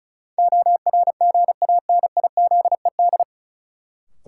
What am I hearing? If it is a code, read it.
Morse code, heard as a keyed tone, about 36 wpm, 700 Hz. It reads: ORGANIZED
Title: Sentences from Top 1300 Words - New Word Drill - 35wpm